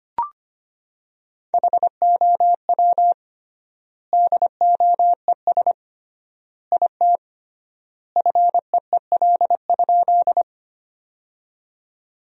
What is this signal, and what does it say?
Morse code, heard as a keyed tone, about 25 wpm, 700 Hz. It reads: HOW DOES IT FEEL?
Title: 3x Word-Spacing - Sentences from Top 400 Words - 25wpm